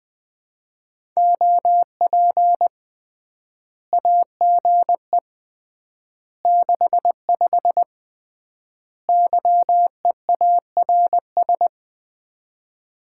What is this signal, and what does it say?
Morse code, heard as a keyed tone, about 20 wpm, 700 Hz. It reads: OP AGE 65 YEARS
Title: Common QSO Phrases - 3x Word-Spacing - 20wpm